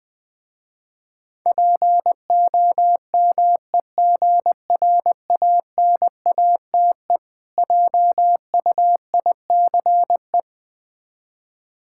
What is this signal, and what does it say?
Morse code, heard as a keyed tone, about 20 wpm, 700 Hz. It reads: POMEGRANATE JUICE
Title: Difficult Words with Hints - 20wpm